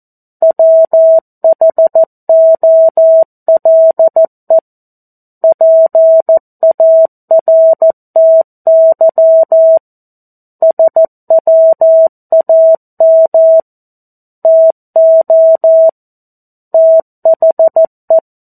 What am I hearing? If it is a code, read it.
Morse code, heard as a keyed tone, about 14 wpm, 650 Hz. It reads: WHOLE PARTY SWAM TO THE
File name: 02-Chapter-02 14 WPM